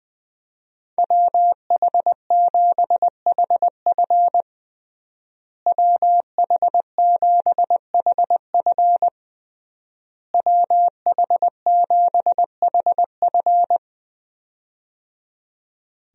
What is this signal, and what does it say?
Morse code, heard as a keyed tone, about 20 wpm, 700 Hz. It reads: WH7HF WH7HF WH7HF
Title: USA Call Signs - 5 Characters - Repeated 3x - 20wpm